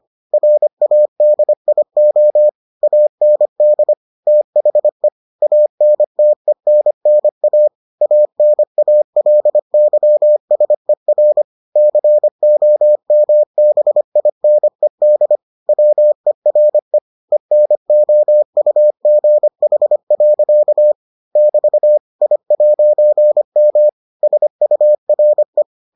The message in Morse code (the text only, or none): RADIO AND THE ANTENNA ANALYSER COMBINED WERE ENOUGH. <BT> I'M SURE